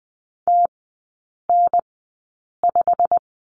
Morse code, heard as a keyed tone, about 20 wpm, spaced as for 8 wpm, 690 Hz. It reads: TN5
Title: 159 Lesson 26e - Letter g - Three Characters 20wpm